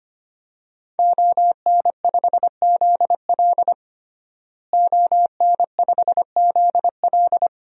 Morse code, heard as a keyed tone, about 25 wpm, 700 Hz. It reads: ON5ZL ON5ZL
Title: DXCC-100 Call Signs - 5 Characters - Repeated 2x - 25wpm